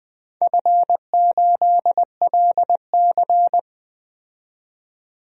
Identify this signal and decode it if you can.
Morse code, heard as a keyed tone, about 20 wpm, 700 Hz. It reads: F8LC
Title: DXCC-100 Call Signs - 4 Characters - 20wpm